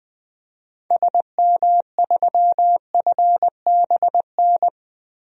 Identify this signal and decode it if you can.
Morse code, heard as a keyed tone, about 20 wpm, 700 Hz. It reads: SM3FBN
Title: DXCC-100 Call Signs - 6 Characters - Repeated 2x - 20wpm